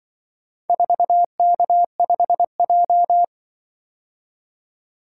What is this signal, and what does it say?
Morse code, heard as a keyed tone, about 24 wpm, 700 Hz. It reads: 4K5J
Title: Speed-Racing - Four Letter-Number Combinations - 20wpm